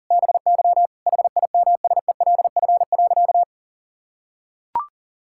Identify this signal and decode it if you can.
Morse code, heard as a keyed tone, about 40 wpm, 700 Hz. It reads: BY HIMSELF.
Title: Sentences from Top 1600 Words 40wpm